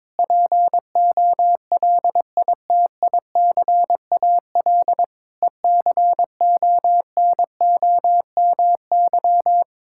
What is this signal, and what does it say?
Morse code, heard as a keyed tone, about 22 wpm, 700 Hz. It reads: POLITICAL ECONOMY
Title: Sentences from Top 1600 Words 22wpm